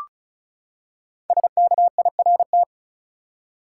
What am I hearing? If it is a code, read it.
Morse code, heard as a keyed tone, about 34 wpm, 700 Hz. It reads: SKIRT
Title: Five Letter Words - 35wpm